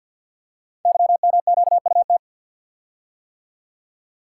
Morse code, heard as a keyed tone, about 48 wpm, 700 Hz. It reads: YMQUT
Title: Code Groups - 5 Letters - 50wpm